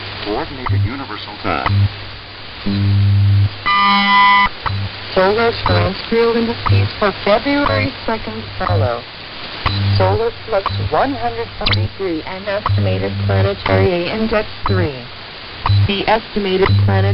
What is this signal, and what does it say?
Radio time signal, with speech.
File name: WWV Timesignal